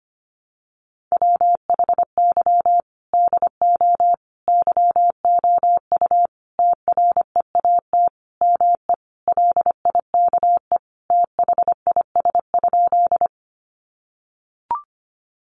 Morse code, heard as a keyed tone, about 25 wpm, 700 Hz. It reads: WHY DO YOU TREAT ME LIKE THIS?
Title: Sentences from Top 1300 Words 25wpm